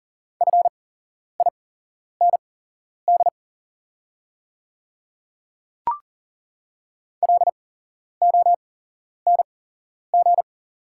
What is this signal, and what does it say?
Morse code, heard as a keyed tone, about 41 wpm, spaced as for 11 wpm, 700 Hz. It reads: FIND LONG
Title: 8x Character Spacing - Top 100 Words - 40wpm